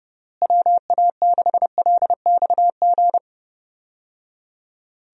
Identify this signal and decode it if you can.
Morse code, heard as a keyed tone, about 32 wpm, 700 Hz. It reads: WA6LXG
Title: USA Call Signs - 6 Characters - Repeated 3x - 30wpm